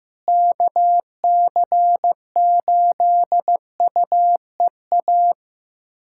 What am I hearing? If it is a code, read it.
Morse code, heard as a keyed tone, about 16 wpm, 700 Hz. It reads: KC8UEA
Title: USA Callsign DE Callsign - 6 Characters - 15wpm